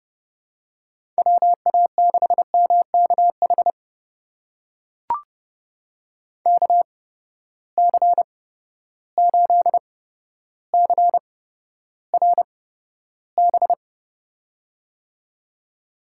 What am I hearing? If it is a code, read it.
Morse code, heard as a keyed tone, about 30 wpm, spaced as for 13 wpm, 700 Hz. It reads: WA6MKH KC8CRB
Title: USA Call Signs - 6 Characters - 8x Character Spacing - 30wpm